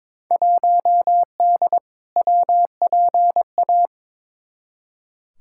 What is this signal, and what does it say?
Morse code, heard as a keyed tone, about 22 wpm, 700 Hz. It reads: 1D WPA
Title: Field Day Exchanges - 22wpm